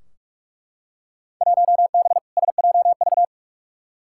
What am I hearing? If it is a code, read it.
Morse code, heard as a keyed tone, about 43 wpm, 700 Hz. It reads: 1B SJV
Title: Field Day Exchanges - 45wpm